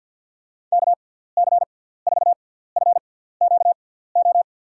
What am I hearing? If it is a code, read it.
Morse code, heard as a keyed tone, about 50 wpm, spaced as for 17 wpm, 700 Hz. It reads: KC4FQO
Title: USA Call Signs - 6x Character Spacing - 50wpm